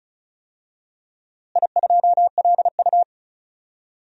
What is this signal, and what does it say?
Morse code, heard as a keyed tone, about 34 wpm, 700 Hz. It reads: I2LU
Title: DXCC-100 Call Signs - 4 Characters - Repeated 2x - 35wpm